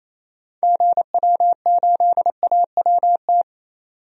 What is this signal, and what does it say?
Morse code, heard as a keyed tone, about 28 wpm, 700 Hz. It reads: GW8AWT
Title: DXCC-100 Call Signs - 6 Characters - 28wpm